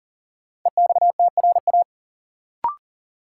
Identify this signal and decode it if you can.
Morse code, heard as a keyed tone, about 37 wpm, 700 Hz. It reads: EXTRA
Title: Sentences from Top 1000 Words - New Word Drill - 40wpm